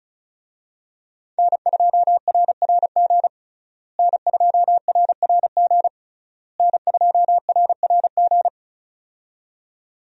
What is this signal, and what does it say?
Morse code, heard as a keyed tone, about 36 wpm, 700 Hz. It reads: N2RRG N2RRG N2RRG
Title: USA Call Signs - 5 Characters - Repeated 3x - 35wpm